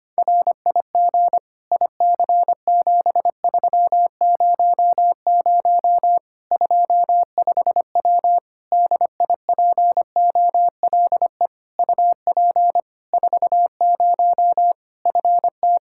Morse code, heard as a keyed tone, about 25 wpm, 700 Hz. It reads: RIG IC7300 25W DIPOLE UP 40 FT
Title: K1USN SST QSO Practice 25wpm